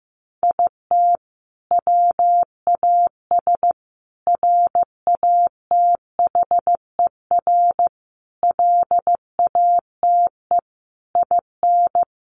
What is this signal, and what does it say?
Morse code, heard as a keyed tone, about 15 wpm, 700 Hz. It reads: IT WAS RATHER LATE IN
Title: Sentences from Top 1400 Words 15wpm